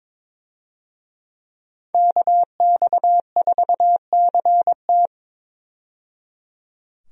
Morse code, heard as a keyed tone, about 21 wpm, 700 Hz. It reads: KX4CT
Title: USA Call Signs - 5 Characters - 22wpm